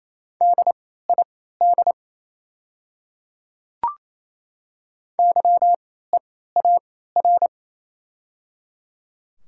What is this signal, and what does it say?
Morse code, heard as a keyed tone, about 28 wpm, spaced as for 16 wpm, 700 Hz. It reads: DID YEAR
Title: 3x Character Spacing - Top 500 Words - 28wpm